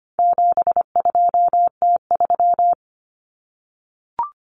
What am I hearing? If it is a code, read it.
Morse code, heard as a keyed tone, about 25 wpm, 700 Hz. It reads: 72T3
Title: Common Cut Numbers - 4-Digit 25wpm